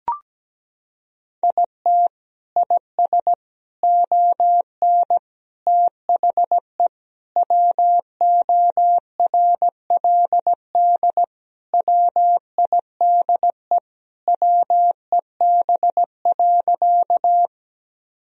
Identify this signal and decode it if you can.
Morse code, heard as a keyed tone, about 17 wpm, 700 Hz. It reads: IT IS ON THE WORLD WIDE WEB.
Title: Sentences from Top 1000 Words - Review Part 4 - 17wpm